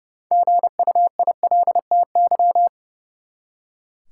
Morse code, heard as a keyed tone, about 32 wpm, 700 Hz. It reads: GUILTY
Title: Six Letter Words - 30wpm